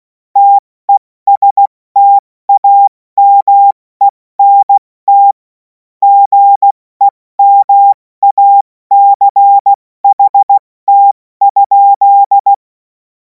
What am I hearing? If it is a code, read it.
Morse code, heard as a keyed tone, about 16 wpm, 800 Hz. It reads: TESTAMENT GEMACHT?
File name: Morsekurs / Witze / Witz 07. Juli 2026 70 F80